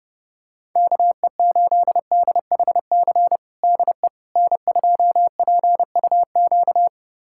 Morse code, heard as a keyed tone, about 30 wpm, 700 Hz. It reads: KE8DHC DE N2PUQ
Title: USA Callsign DE Callsign 30wpm